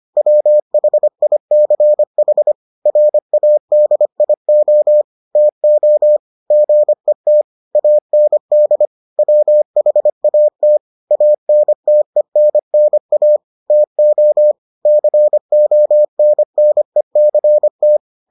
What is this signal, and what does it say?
Morse code, heard as a keyed tone, about 25 wpm, 600 Hz. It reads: WHICH RADIO TO GET AND WHAT ANTENNA TO CONNECT